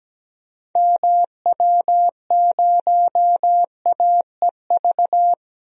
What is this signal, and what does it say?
Morse code, heard as a keyed tone, about 17 wpm, 700 Hz. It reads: MW0AEV
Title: DXCC-100 Call Signs - 6 Characters - 17wpm